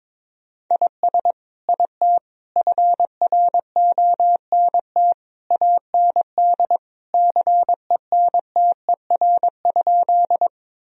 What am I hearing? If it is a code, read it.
Morse code, heard as a keyed tone, about 22 wpm, 700 Hz. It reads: IS IT FRONT AND CENTER?
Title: Sentences from Top 1000 Words - Review Part 2 - 22wpm